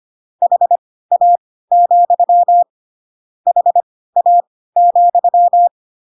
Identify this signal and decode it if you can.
Morse code, heard as a keyed tone, about 25 wpm, spaced as for 16 wpm, 700 Hz. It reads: HA, HA,
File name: Pinocchio / Pinocchio ch1 10 25 ik1hge